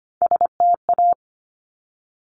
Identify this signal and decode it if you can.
Morse code, heard as a keyed tone, about 25 wpm, 700 Hz. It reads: STA
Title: Top Three Letter Combinations in Words 25wpm